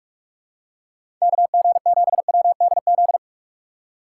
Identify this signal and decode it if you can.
Morse code, heard as a keyed tone, about 46 wpm, 700 Hz. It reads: KG7WDZ
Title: USA Call Signs - 6 Characters - 2x Character Spacing - 45wpm